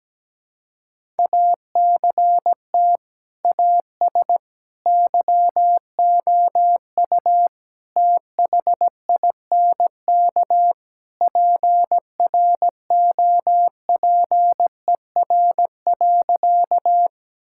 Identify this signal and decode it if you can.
Morse code, heard as a keyed tone, about 17 wpm, 700 Hz. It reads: ACT AS YOU THINK PROPER.
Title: Sentences from Top 1000 Words - Review Part 5 - 17wpm